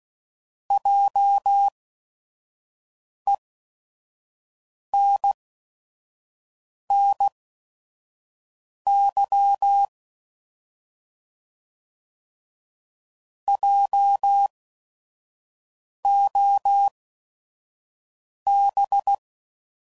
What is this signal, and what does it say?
Morse code, heard as a keyed tone, about 16 wpm, spaced as for 5 wpm, 800 Hz. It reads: JENNY JOB